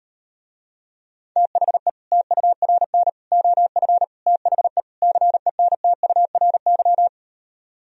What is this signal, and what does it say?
Morse code, heard as a keyed tone, about 38 wpm, 700 Hz. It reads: THE TURN OF THE CENTURY